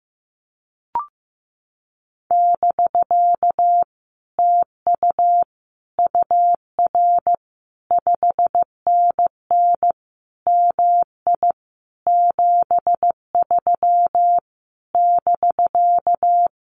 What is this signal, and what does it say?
Morse code, heard as a keyed tone, about 15 wpm, 700 Hz. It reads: <BK> TU UR 5NN MI 73 <BK>
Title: Common QSO Phrases Encore 15wpm